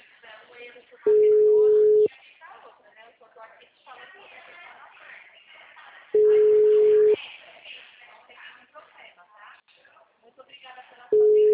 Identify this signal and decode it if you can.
Ringback tone (Europe, 425 Hz), with speech.
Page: Dracena (Call-Center)